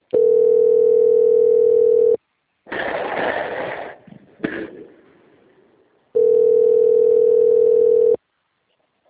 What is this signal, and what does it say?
Ringback tone (North America), 440 + 480 Hz.